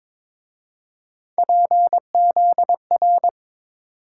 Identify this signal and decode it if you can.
Morse code, heard as a keyed tone, about 27 wpm, 700 Hz. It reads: PZR
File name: Speed-Racing - Three Letter-Number Combinations - 22wpm